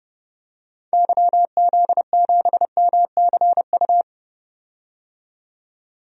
Morse code, heard as a keyed tone, about 30 wpm, 700 Hz. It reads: YZ7MCU
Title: DXCC-100 Call Signs - 6 Characters - 30wpm